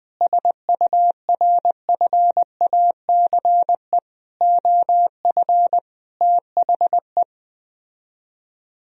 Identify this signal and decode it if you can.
Morse code, heard as a keyed tone, about 20 wpm, 700 Hz. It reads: SURFACE OF THE
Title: Speed-Racing - Sets of 4 Words 17wpm